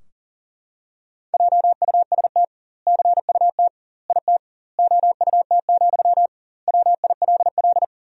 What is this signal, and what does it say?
Morse code, heard as a keyed tone, about 40 wpm, 700 Hz. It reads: JUST CUT IT OUT, WILL
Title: 2x Word-Spacing - Sentences from Top 300 Words - 40wpm